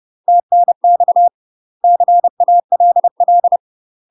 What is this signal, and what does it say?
Morse code, heard as a keyed tone, about 30 wpm, 700 Hz. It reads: TNX CALL
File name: UK MASTER QSOs TRAINING / 30WPM / UK MASTER QSOs 0027